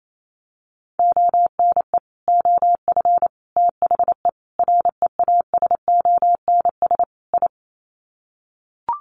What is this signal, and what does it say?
Morse code, heard as a keyed tone, about 28 wpm, 700 Hz. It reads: ONE OF THE REASONS I
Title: Sets of 5 Words Encore2 28wpm